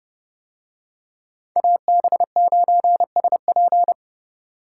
Morse code, heard as a keyed tone, about 30 wpm, 700 Hz. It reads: AB9SP